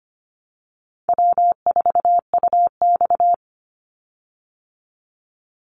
Morse code, heard as a keyed tone, about 25 wpm, 700 Hz. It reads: W4UX
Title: Easy Call Signs Repeated 2x - 25wpm